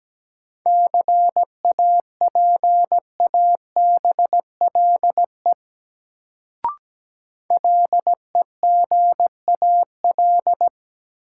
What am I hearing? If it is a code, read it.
Morse code, heard as a keyed tone, about 17 wpm, 700 Hz. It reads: CAPABLE LEGAL
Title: Sentences from Top 1200 Words - New Word Drill Alt - 17wpm